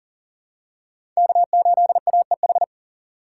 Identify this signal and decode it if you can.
Morse code, heard as a keyed tone, about 40 wpm, 700 Hz. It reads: K8AEH
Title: USA Call Signs - 5 Characters - 4x Character Spacing - 40wpm